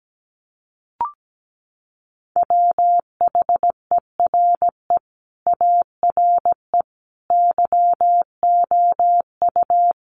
Morse code, heard as a keyed tone, about 17 wpm, 700 Hz. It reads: WHERE ARE YOU
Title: Sentences from Top 1100 Words 17wpm